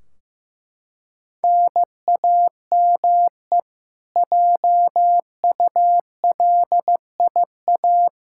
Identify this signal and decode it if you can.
Morse code, heard as a keyed tone, about 15 wpm, 700 Hz. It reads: NAME JULIA
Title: Practice QSO - Exchange 1 - Beginning - 15wpm